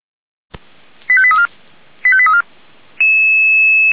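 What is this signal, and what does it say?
Five-tone selective call (ZVEI), heard as quick runs of five tones, tones of about 80 ms. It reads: 85834 85834
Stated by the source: ZVEI five-tone selective call received by SDR.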